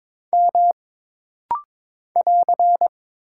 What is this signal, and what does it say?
Morse code, heard as a keyed tone, about 22 wpm, 700 Hz. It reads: M <AR>
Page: This rapid-fire timing is geared to push your ICR (Instant Character Recognition) skills.